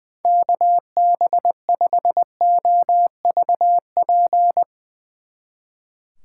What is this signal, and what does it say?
Morse code, heard as a keyed tone, about 22 wpm, 700 Hz. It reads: KB5OVP
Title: Speed Racing - USA 2x3 Call Signs 20wpm